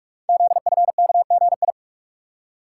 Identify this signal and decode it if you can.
Morse code, heard as a keyed tone, about 44 wpm, 690 Hz. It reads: ZFKGI
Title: Code Groups - 5 Letters - 45wpm